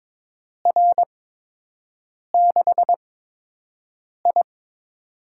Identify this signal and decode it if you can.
Morse code, heard as a keyed tone, about 21 wpm, spaced as for 6 wpm, 700 Hz. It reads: R6I
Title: USA Call Signs - 8x Character Spacing - 22wpm